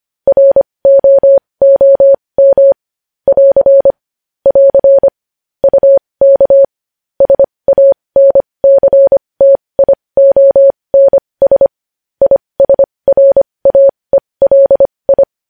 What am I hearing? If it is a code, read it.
Morse code, heard as a keyed tone, about 25 wpm, spaced as for 20 wpm, 550 Hz. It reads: ROOM <AR> <AR> UK SANCTIONS ISRAELI
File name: podcast / News headlines-1x-25x20